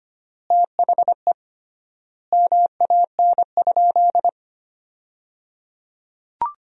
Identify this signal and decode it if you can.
Morse code, heard as a keyed tone, about 25 wpm, 700 Hz. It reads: THE MAN?
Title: 3x Word-Spacing - Sentences from Top 200 Words - 25wpm